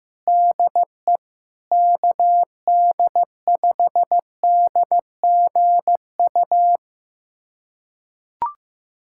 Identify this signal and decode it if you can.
Morse code, heard as a keyed tone, about 15 wpm, 700 Hz. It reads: DE KD5DGU
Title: USA Callsign DE Callsign 15wpm